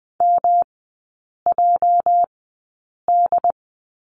Morse code, heard as a keyed tone, about 20 wpm, spaced as for 8 wpm, 700 Hz. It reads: MJD
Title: 205 Lesson 33e - Letter j - Three Characters 20wpm